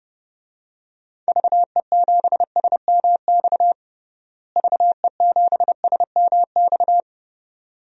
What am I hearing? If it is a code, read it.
Morse code, heard as a keyed tone, about 32 wpm, 700 Hz. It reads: VE7SMX VE7SMX
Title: DXCC-100 Call Signs - 6 Characters - Repeated 2x - 30wpm